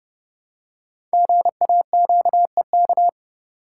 Morse code, heard as a keyed tone, about 28 wpm, 700 Hz. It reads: GAQEK
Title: Code Groups - 5 Letters - 30wpm